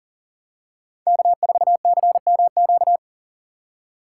Morse code, heard as a keyed tone, about 40 wpm, 700 Hz. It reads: K4CMQ
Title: USA Call Signs - 5 Characters - 3x Character Spacing - 40wpm